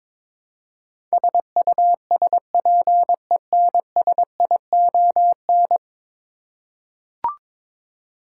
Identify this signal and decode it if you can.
Morse code, heard as a keyed tone, about 22 wpm, 700 Hz. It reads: SUSPENSION
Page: morse-code-ninja-practice-22wpm